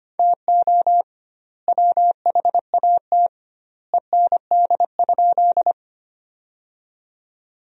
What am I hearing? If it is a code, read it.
Morse code, heard as a keyed tone, about 25 wpm, 700 Hz. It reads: TO WHAT END?
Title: 2x Word-Spacing - Sentences from Top 200 Words - 25wpm